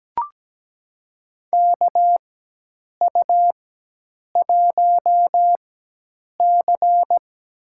Morse code, heard as a keyed tone, about 17 wpm, spaced as for 8 wpm, 700 Hz. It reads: KU1C
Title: USA Call Signs - 4 Characters - 4x Character Spacing - 17wpm